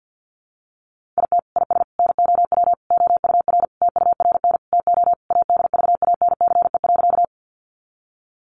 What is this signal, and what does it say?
Morse code, heard as a keyed tone, about 50 wpm, 700 Hz. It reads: IT IS NOW OUR TURN TO ADVANCE.
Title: Sentences from Top 1400 Words 50wpm